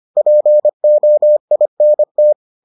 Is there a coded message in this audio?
Yes: Morse code from 0.2 to 2.3 s, heard as a keyed tone, about 25 wpm, 600 Hz. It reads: POINT